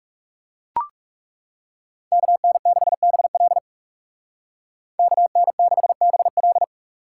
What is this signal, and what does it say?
Morse code, heard as a keyed tone, about 37 wpm, 700 Hz. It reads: KN6BL KN6BL
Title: Speed Racing - USA Call Signs 30wpm